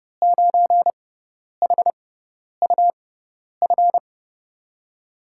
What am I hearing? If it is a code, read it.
Morse code, heard as a keyed tone, about 30 wpm, spaced as for 10 wpm, 700 Hz. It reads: 9HUF
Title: USA Call Signs - 5 Characters - 6x Character Spacing - 30wpm